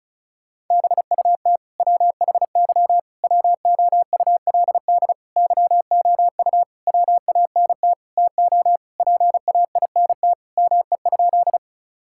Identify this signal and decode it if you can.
Morse code, heard as a keyed tone, about 34 wpm, 700 Hz. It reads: BUT WHY WOULD YOU WANT TO PAINT ME?
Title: Sentences from Top 1000 Words - Review Part 6 - 35wpm